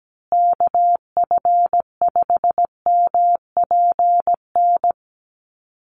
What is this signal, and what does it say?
Morse code, heard as a keyed tone, about 17 wpm, 700 Hz. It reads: KF5MPN
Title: USA Call Signs Repeated 2x 17wpm